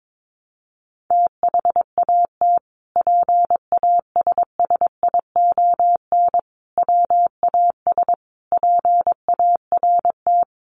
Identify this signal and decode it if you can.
Morse code, heard as a keyed tone, about 22 wpm, 700 Hz. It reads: THAT PASSION WAS PART
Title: Sentences from Top 1300 Words 22wpm